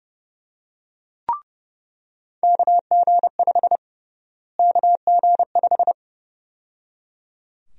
Morse code, heard as a keyed tone, about 32 wpm, 700 Hz. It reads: KG5 KG5
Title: USA Call Sign Prefixes - Repeated 2x - 30wpm